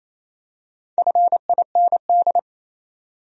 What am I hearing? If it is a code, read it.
Morse code, heard as a keyed tone, about 28 wpm, 700 Hz. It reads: FIND
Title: Top 100 Words 28wpm